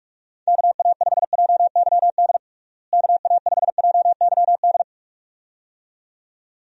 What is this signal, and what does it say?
Morse code, heard as a keyed tone, about 46 wpm, 700 Hz. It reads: KA5JYD KA5JYD
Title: USA Call Signs - 6 Characters - Repeated 2x - 45wpm